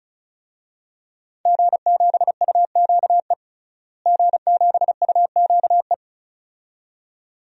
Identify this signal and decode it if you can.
Morse code, heard as a keyed tone, about 36 wpm, 700 Hz. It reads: G7UQE G7UQE
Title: DXCC-100 Call Signs - 5 Characters - Repeated 2x - 35wpm